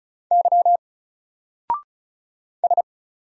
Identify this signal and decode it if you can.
Morse code, heard as a keyed tone, about 36 wpm, 700 Hz. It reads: Y S
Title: Single Letter-Number Alt 35wpm